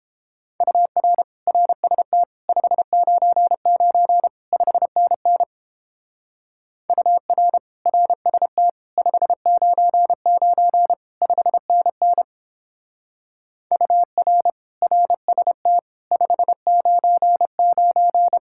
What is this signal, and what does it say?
Morse code, heard as a keyed tone, about 31 wpm, 700 Hz. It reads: UR RST 599 5NN UR RST 599 5NN UR RST 599